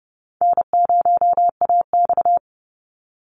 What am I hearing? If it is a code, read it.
Morse code, heard as a keyed tone, about 30 wpm, 700 Hz. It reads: N0AX
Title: Easy Call Signs - 30wpm